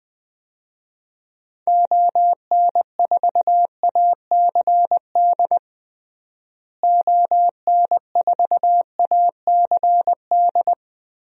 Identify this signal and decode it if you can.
Morse code, heard as a keyed tone, about 20 wpm, 700 Hz. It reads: ON4ACD ON4ACD
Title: DXCC-100 Call Signs - 6 Characters - Repeated 2x - 20wpm